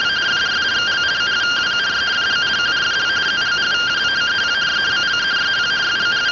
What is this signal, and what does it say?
Two-tone teleprinter signal, about 170 Hz shift.
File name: findings / CT 205 ax mips, iDose 3 0006.gen-09 rtty rx